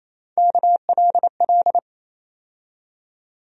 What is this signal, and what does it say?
Morse code, heard as a keyed tone, about 28 wpm, 700 Hz. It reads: KLL
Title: USA Call Sign Postfixes - 3 Letters 28wpm